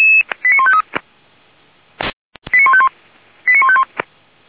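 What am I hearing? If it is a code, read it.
Five-tone selective call (ZVEI), heard as quick runs of five tones, tones of about 80 ms. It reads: 89153 89151 89151
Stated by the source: ZVEI five-tone selective call received with an SDR.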